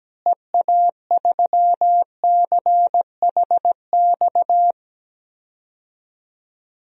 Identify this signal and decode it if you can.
Morse code, heard as a keyed tone, about 17 wpm, 700 Hz. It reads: EA3CHX
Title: DXCC-100 Call Signs - 6 Characters - Repeated 2x - 17wpm